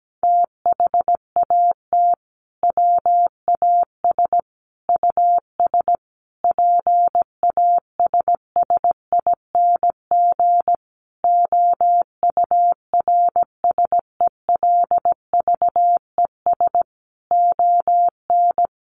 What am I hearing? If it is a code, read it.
Morse code, heard as a keyed tone, about 17 wpm, 700 Hz. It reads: THAT WAS US PASSING OURSELVES ON
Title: A Sound Of Thunder 17wpm